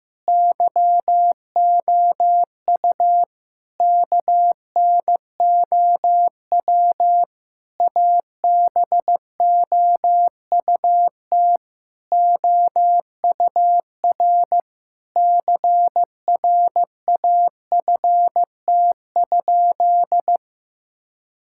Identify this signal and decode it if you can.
Morse code, heard as a keyed tone, about 16 wpm, 700 Hz. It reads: YOU KNOW ABOUT OUR CRAFT?
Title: Sentences from Top 1000 Words - Review Part 6 - 15wpm